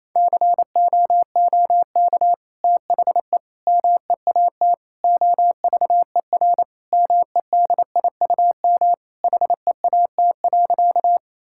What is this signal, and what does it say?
Morse code, heard as a keyed tone, about 28 wpm, 700 Hz. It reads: COOK THE MEAT OVER MEDIUM HEAT.
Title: Sentences from Top 1000 Words - Review Part 6 - 28wpm